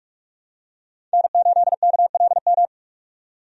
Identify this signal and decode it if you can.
Morse code, heard as a keyed tone, about 46 wpm, 700 Hz. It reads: N8KLM
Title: USA Call Signs - 5 Characters - 45wpm